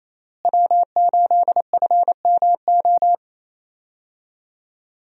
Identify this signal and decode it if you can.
Morse code, heard as a keyed tone, about 28 wpm, 700 Hz. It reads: W8FMO
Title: USA Call Signs - 5 Characters - Repeated 2x - 28wpm